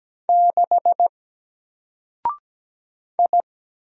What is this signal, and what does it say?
Morse code, heard as a keyed tone, about 17 wpm, 700 Hz. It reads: 6 I
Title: Single Character (Letter:Number:Punctuation:Prosign) 17wpm